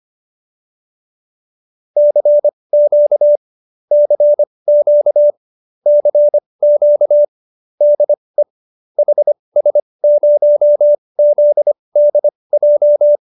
Morse code, heard as a keyed tone, about 25 wpm, spaced as for 20 wpm, 590 Hz. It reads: CQ CQ CQ DE HS0ZDJ
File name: qso216 20